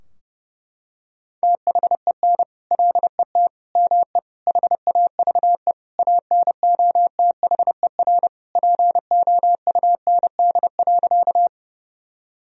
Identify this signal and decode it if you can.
Morse code, heard as a keyed tone, about 28 wpm, 700 Hz. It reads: THEN LET ME HAVE ANOTHER POUND.
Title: Sentences from Top 1000 Words - Review Part 2 - 30wpm